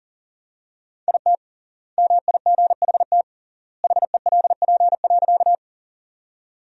Morse code, heard as a keyed tone, about 40 wpm, 700 Hz. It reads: IT MIGHT HELP.
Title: 3x Word-Spacing - Sentences from Top 300 Words - 40wpm